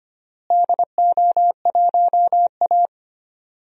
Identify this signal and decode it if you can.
Morse code, heard as a keyed tone, about 25 wpm, 700 Hz. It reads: DO1A
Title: DXCC-100 Call Signs - 4 Characters - Repeated 3x - 25wpm